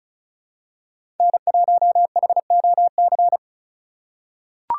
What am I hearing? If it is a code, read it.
Morse code, heard as a keyed tone, about 36 wpm, 700 Hz. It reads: N1HOC
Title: USA Call Signs - 5 Characters - 4x Character Spacing - 35wpm